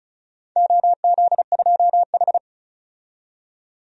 Morse code, heard as a keyed tone, about 36 wpm, 700 Hz. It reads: OZ2H
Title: DXCC-100 Call Signs - 4 Characters - 35wpm